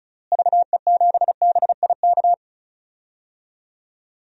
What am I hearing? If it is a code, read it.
Morse code, heard as a keyed tone, about 36 wpm, 700 Hz. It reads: VE7BIK
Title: DXCC-100 Call Signs - 6 Characters - Repeated 3x - 35wpm